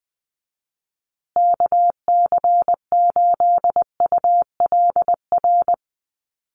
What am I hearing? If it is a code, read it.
Morse code, heard as a keyed tone, about 20 wpm, 700 Hz. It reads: KC8ULR
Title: USA Call Signs Repeated 2x 20wpm